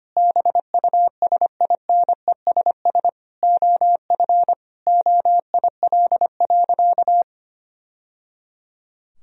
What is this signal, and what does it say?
Morse code, heard as a keyed tone, about 25 wpm, 700 Hz. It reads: BUSINESS OF OIL.
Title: Sentences from Top 1000 Words - Review Part 4 - 25wpm